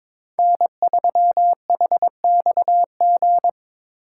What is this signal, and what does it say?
Morse code, heard as a keyed tone, about 22 wpm, 700 Hz. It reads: N3HXG
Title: USA Call Signs - 5 Characters - 3x Character Spacing - 22wpm